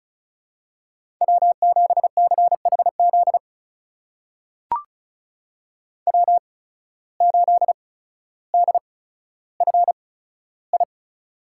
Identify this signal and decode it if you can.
Morse code, heard as a keyed tone, about 36 wpm, spaced as for 15 wpm, 700 Hz. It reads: W7CHZ W8DFI
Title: USA Call Signs - 5 Characters - 8x Character Spacing - 35wpm